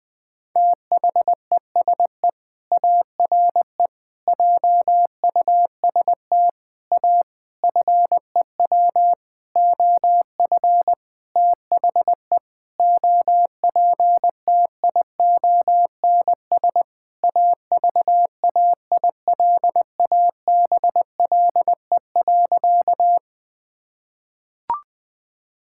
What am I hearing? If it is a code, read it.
Morse code, heard as a keyed tone, about 20 wpm, 700 Hz. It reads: THESE ARE JUST A FEW OF THE OPTIONS AVAILABLE.
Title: Sentences with 7-Word Phrases - 20wpm